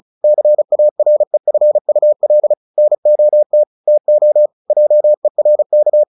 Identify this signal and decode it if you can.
Morse code, heard as a keyed tone, about 34 wpm, 600 Hz. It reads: CAREFUL NOT TO JERK
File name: the old man and the sea 2 35wpm